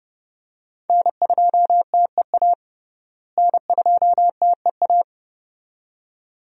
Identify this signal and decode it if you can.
Morse code, heard as a keyed tone, about 30 wpm, 700 Hz. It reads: N2TEA N2TEA
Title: USA Call Signs - 5 Characters - Repeated 2x - 30wpm